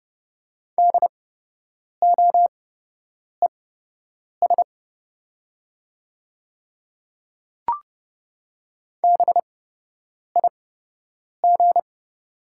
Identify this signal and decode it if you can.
Morse code, heard as a keyed tone, about 30 wpm, spaced as for 8 wpm, 700 Hz. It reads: DOES BIG
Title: 8x Character Spacing - Top 200 Words - 30wpm